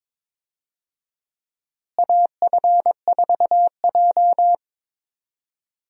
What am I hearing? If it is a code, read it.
Morse code, heard as a keyed tone, about 21 wpm, 700 Hz. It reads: AF4J
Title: USA Call Signs - 4 Characters - Repeated 2x - 22wpm